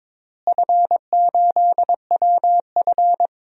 Morse code, heard as a keyed tone, about 21 wpm, 700 Hz. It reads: F8WF
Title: DXCC-100 Call Signs - 4 Characters - Repeated 2x - 22wpm